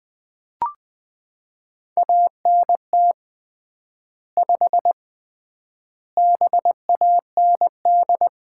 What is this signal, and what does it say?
Morse code, heard as a keyed tone, about 20 wpm, 700 Hz. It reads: ANT 5 BAND
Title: Common QSO Phrases - 3x Word-Spacing - 20wpm